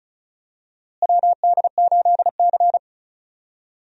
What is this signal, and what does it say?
Morse code, heard as a keyed tone, about 36 wpm, 700 Hz. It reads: WD8C
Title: USA Call Signs - 4 Characters - Repeated 2x - 35wpm